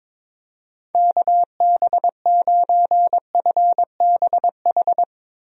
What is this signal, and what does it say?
Morse code, heard as a keyed tone, about 22 wpm, 700 Hz. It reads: KB9FBH
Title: USA Call Signs - 6 Characters - Repeated 2x - 22wpm